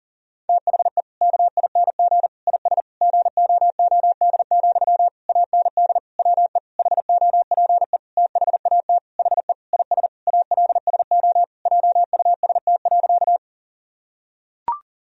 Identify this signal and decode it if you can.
Morse code, heard as a keyed tone, about 40 wpm, 700 Hz. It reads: THE KING IS GOOD, AND WE HOPE THAT HE IS ALSO JUST.
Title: Sentences from Top 1000 Words - Review Part 1 - 40wpm